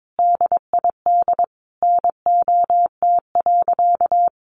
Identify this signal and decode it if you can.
Morse code, heard as a keyed tone, about 22 wpm, 700 Hz. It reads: DID NOT.
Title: Sentences from Top 600 Words 22wpm